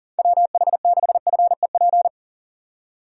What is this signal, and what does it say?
Morse code, heard as a keyed tone, about 40 wpm, 700 Hz. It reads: WH6FEP
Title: USA Call Signs - 6 Characters - 40wpm